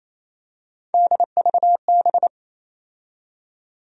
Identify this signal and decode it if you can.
Morse code, heard as a keyed tone, about 28 wpm, 700 Hz. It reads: DVB
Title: USA Call Sign Postfixes - 3 Letters 28wpm